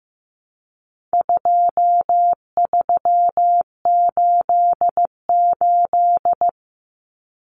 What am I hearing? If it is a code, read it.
Morse code, heard as a keyed tone, about 15 wpm, 700 Hz. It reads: 2388
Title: Numbers - 4-Digit 15wpm